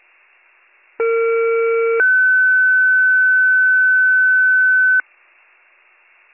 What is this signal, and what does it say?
Two-tone page, heard as a one-second tone, then a three-second tone, about 460 then 1500 Hz. It is heard from 1.0 to 5.0 s.